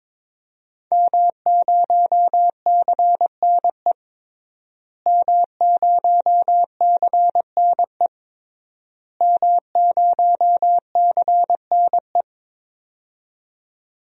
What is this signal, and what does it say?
Morse code, heard as a keyed tone, about 22 wpm, 700 Hz. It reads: M0CNE M0CNE M0CNE
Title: DXCC-100 Call Signs - 5 Characters - Repeated 3x - 22wpm